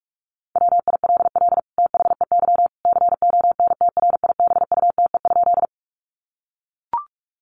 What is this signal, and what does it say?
Morse code, heard as a keyed tone, about 44 wpm, 700 Hz. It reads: WILL THEY CONTRIBUTE?
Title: Sentences from Top 1400 Words 45wpm